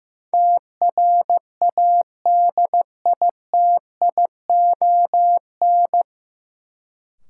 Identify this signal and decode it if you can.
Morse code, heard as a keyed tone, about 15 wpm, 700 Hz. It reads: TRADITION
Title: Sentences from Top 1500 Words - New Word Drill Alt - 15wpm